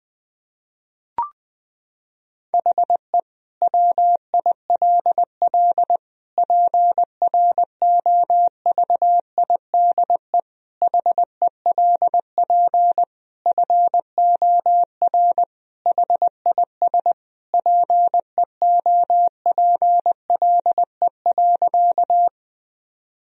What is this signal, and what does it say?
Morse code, heard as a keyed tone, about 20 wpm, 700 Hz. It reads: HE WILL PROVIDE HELP FOR HIS PEOPLE.
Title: Sentences from Top 1000 Words - Review Part 3 - 20wpm